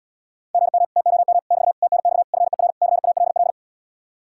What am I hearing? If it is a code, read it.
Morse code, heard as a keyed tone, about 79 wpm, 700 Hz. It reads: GM TOM <BT> TNX FER CALL